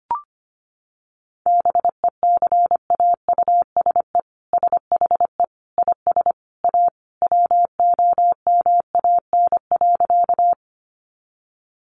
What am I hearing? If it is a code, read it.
Morse code, heard as a keyed tone, about 25 wpm, 700 Hz. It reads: BECAUSE SHE IS A WOMAN.
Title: Sentences from Top 1100 Words 25wpm